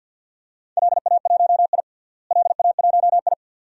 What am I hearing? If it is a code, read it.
Morse code, heard as a keyed tone, about 47 wpm, 700 Hz. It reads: LA1I LA1I
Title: DXCC-100 Call Signs - 4 Characters - Repeated 2x - 50wpm